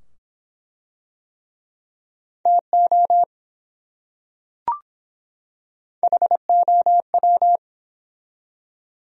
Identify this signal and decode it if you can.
Morse code, heard as a keyed tone, about 23 wpm, 700 Hz. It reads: TO HOW